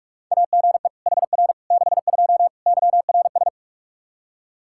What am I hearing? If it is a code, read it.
Morse code, heard as a keyed tone, about 44 wpm, 700 Hz. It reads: AGE HR 62 YRS
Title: Practice QSO - Exchange 3 - More Info - 45wpm